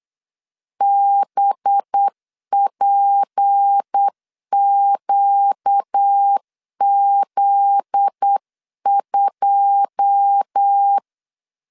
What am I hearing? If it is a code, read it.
Morse code, heard as a keyed tone, about 8 wpm, 800 Hz. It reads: BPQZ2
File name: LEZIONI / 9 2 8wpm 3437